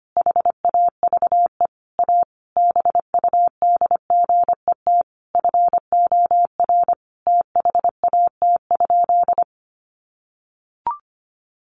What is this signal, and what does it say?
Morse code, heard as a keyed tone, about 25 wpm, 700 Hz. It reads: HAVE A BUDGET FOR THAT?
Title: Sentences from Top 1200 Words 25wpm